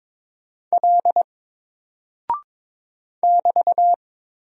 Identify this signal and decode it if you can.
Morse code, heard as a keyed tone, about 22 wpm, 700 Hz. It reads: L <BT>
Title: Single Character (Letter:Number:Punctuation:Prosign) 22wpm